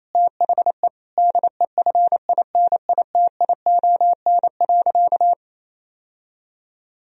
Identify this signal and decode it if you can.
Morse code, heard as a keyed tone, about 28 wpm, 700 Hz. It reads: THE DEFINITION.
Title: Sentences from Top 1000 Words - Review Part 4 - 28wpm